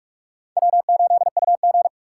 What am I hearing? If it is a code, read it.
Morse code, heard as a keyed tone, about 48 wpm, 700 Hz. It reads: W8UG